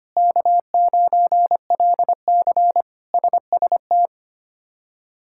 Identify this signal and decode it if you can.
Morse code, heard as a keyed tone, about 25 wpm, 700 Hz. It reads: K9LC SST
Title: K1USN SST QSO Practice - 25wpm